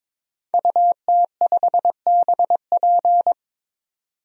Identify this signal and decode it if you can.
Morse code, heard as a keyed tone, about 22 wpm, 700 Hz. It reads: UT5BP
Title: DXCC-100 Call Signs - 5 Characters - 22wpm